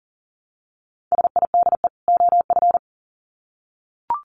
Morse code, heard as a keyed tone, about 40 wpm, 700 Hz. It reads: SIDE OF
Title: Sets of 2 Words 40wpm